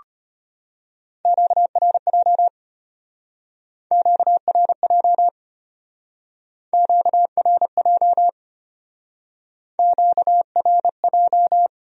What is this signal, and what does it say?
Morse code, heard as a keyed tone, about 32 wpm, 700 Hz. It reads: QRJ QRJ QRJ QRJ